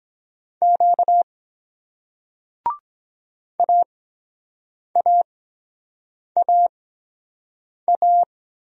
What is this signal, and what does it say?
Morse code, heard as a keyed tone, about 23 wpm, spaced as for 7 wpm, 700 Hz. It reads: Q AAAA